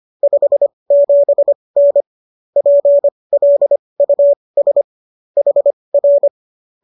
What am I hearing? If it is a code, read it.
Morse code, heard as a keyed tone, about 25 wpm, spaced as for 20 wpm, 570 Hz. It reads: 57N PLUS HR